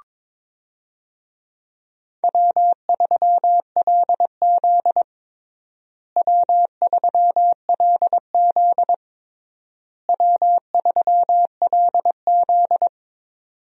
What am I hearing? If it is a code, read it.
Morse code, heard as a keyed tone, about 23 wpm, 700 Hz. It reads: W3LZ W3LZ W3LZ
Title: USA Call Signs - 4 Characters - Repeated 3x - 22wpm